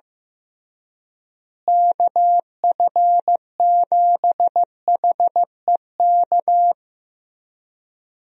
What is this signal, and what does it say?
Morse code, heard as a keyed tone, about 15 wpm, 700 Hz. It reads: KF7HEK
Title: USA Call Signs - 6 Characters - Repeated 3x - 15wpm